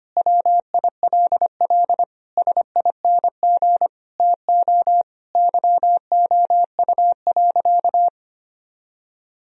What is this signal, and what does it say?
Morse code, heard as a keyed tone, about 25 wpm, 700 Hz. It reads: WILL SING TO YOU.
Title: Sentences from Top 1000 Words - Review Part 2 - 25wpm